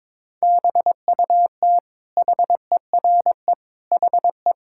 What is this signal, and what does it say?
Morse code, heard as a keyed tone, about 21 wpm, 700 Hz. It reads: BUT HERE HE
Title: Sentences from Top 1600 Words 22wpm